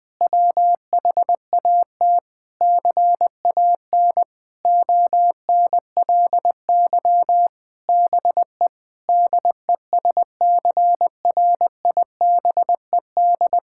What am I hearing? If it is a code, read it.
Morse code, heard as a keyed tone, about 21 wpm, 700 Hz. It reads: WHAT CAN ONLY BE DESCRIBED
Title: Sentences with 8-Word Phrases - 20wpm